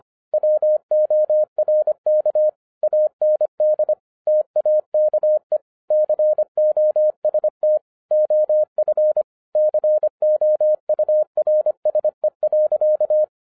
Morse code, heard as a keyed tone, about 24 wpm, 610 Hz. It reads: WORK AND TAKE COST OF COURSE.